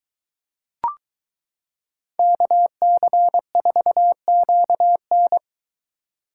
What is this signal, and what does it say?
Morse code, heard as a keyed tone, about 22 wpm, 700 Hz. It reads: KC4QN